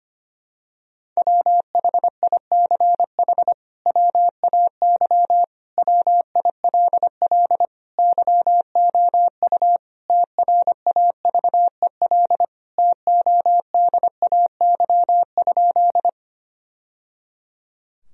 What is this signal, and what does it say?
Morse code, heard as a keyed tone, about 25 wpm, 700 Hz. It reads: WHICH WAY WILL YOU TRAVEL TODAY?
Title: Sentences from Top 1000 Words - Review Part 2 - 25wpm